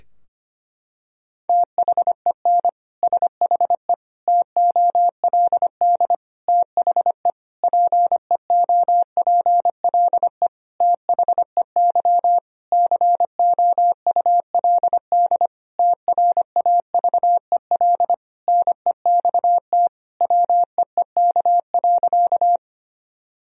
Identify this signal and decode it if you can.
Morse code, heard as a keyed tone, about 25 wpm, 700 Hz. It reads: THEN SHE TOLD THE PEOPLE THEY COULD TRAVEL NEXT WEEK.
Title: Sentences from Top 1000 Words - Review Part 2 - 25wpm